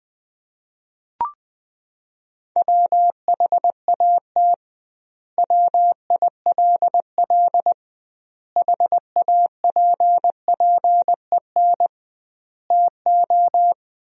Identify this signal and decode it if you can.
Morse code, heard as a keyed tone, about 20 wpm, 700 Hz. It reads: WHAT WILL HAPPEN TO
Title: 2x Word-Spacing - Sentences from Top 400 Words - 20wpm